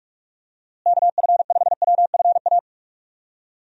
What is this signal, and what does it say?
Morse code, heard as a keyed tone, about 46 wpm, 700 Hz. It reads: KF5WFA
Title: USA Call Signs - 6 Characters - Repeated 2x - 45wpm